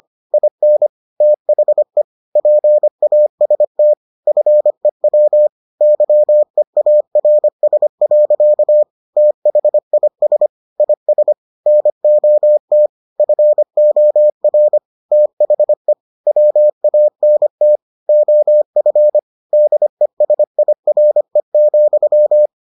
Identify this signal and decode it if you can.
Morse code, heard as a keyed tone, about 25 wpm, 600 Hz. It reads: IN THE PAST FEW YEARS. THIS IS NOT FOR THE WANT OF DESIRE,